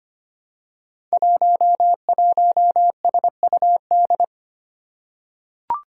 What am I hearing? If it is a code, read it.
Morse code, heard as a keyed tone, about 25 wpm, 700 Hz. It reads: 11SUD
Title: Five Challenging Letter-Number Combinations 25wpm